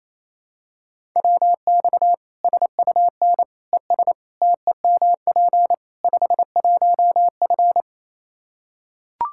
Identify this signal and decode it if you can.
Morse code, heard as a keyed tone, about 28 wpm, 700 Hz. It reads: WX SUN ES TEMP 51F
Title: Practice QSO - Exchange 2 - Info - 28wpm